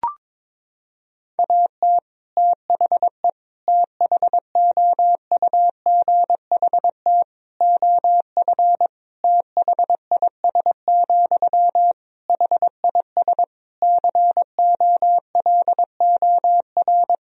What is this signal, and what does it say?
Morse code, heard as a keyed tone, about 22 wpm, 700 Hz. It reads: AT THE THOUGHT OF THIS, HIS COLOR
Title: Sentences from Top 1000 Words - Review Part 2 - 22wpm